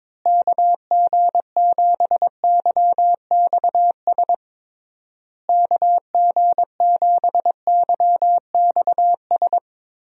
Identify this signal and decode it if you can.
Morse code, heard as a keyed tone, about 22 wpm, 700 Hz. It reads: KG7YXS KG7YXS
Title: USA Call Signs - 6 Characters - Repeated 3x - 22wpm